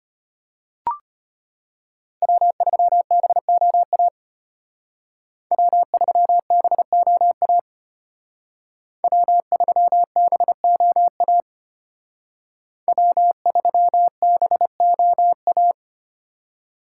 Morse code, heard as a keyed tone, about 31 wpm, 700 Hz. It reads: W3BOA W3BOA W3BOA W3BOA
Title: Speed Racing - USA 1x3 Call Signs 25wpm